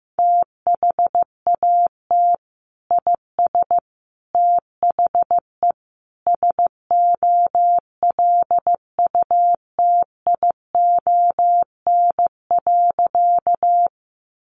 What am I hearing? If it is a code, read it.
Morse code, heard as a keyed tone, about 15 wpm, 700 Hz. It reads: THAT IS THE SOLUTION.
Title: Sentences from Top 1300 Words 15wpm